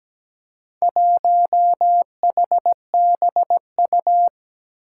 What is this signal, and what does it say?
Morse code, heard as a keyed tone, about 17 wpm, 700 Hz. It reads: 1HBU
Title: Four Challenging Letter-Number Combinations 17wpm